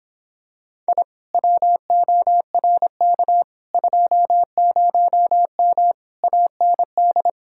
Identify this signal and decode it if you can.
Morse code, heard as a keyed tone, about 26 wpm, 700 Hz. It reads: I WORK 20M AND